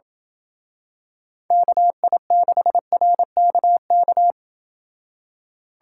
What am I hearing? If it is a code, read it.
Morse code, heard as a keyed tone, about 27 wpm, 700 Hz. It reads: KI6RKK